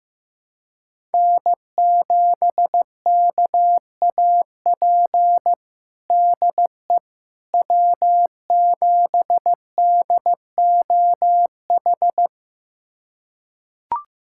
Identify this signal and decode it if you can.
Morse code, heard as a keyed tone, about 15 wpm, 700 Hz. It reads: N7KAP DE W7DOH
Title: USA Callsign DE Callsign - 5 Characters - 15wpm